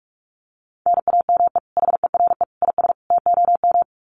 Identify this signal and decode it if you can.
Morse code, heard as a keyed tone, about 44 wpm, 700 Hz. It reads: NAME HERE IS TOM
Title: Common QSO Phrases 45wpm